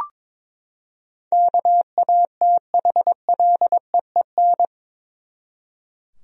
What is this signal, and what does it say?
Morse code, heard as a keyed tone, about 22 wpm, 700 Hz. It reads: KATHLEEN
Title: USA Common Names - 22wpm